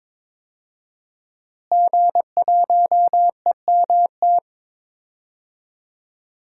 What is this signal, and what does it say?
Morse code, heard as a keyed tone, about 22 wpm, 700 Hz. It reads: G1EMT
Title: DXCC-100 Call Signs - 5 Characters - Repeated 2x - 22wpm